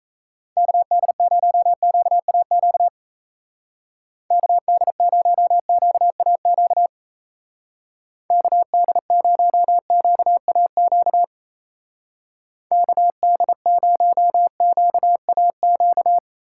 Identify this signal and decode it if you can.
Morse code, heard as a keyed tone, about 36 wpm, 700 Hz. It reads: KD0QAQ KD0QAQ KD0QAQ KD0QAQ